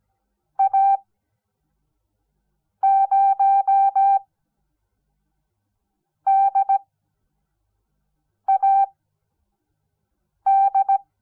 Morse code, heard as a keyed tone, about 17 wpm, spaced as for 4 wpm, 780 Hz. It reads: A0DAD
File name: Onlinelehrgang / morsen / DJ4UF Morsen06